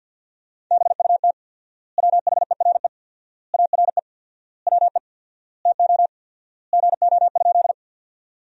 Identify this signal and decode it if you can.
Morse code, heard as a keyed tone, about 50 wpm, 700 Hz. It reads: BUT WHERE ARE WE TO GO?
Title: 4x Word-Spacing - Sentences from Top 200 Words - 50wpm